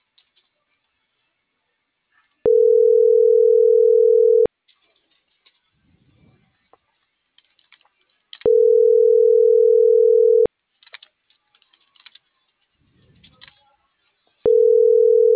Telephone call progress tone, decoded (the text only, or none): ringback tone